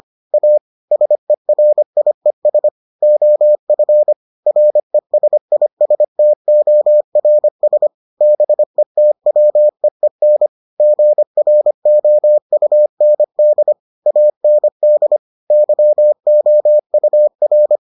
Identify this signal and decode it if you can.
Morse code, heard as a keyed tone, about 25 wpm, 600 Hz. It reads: A SERIES OF RESISTORS BETWEEN GROUND AND YOUR